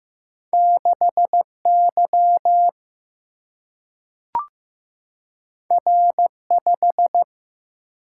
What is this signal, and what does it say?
Morse code, heard as a keyed tone, about 15 wpm, 700 Hz. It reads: 6 Y R 5
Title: Single-Letter Single-Number 15wpm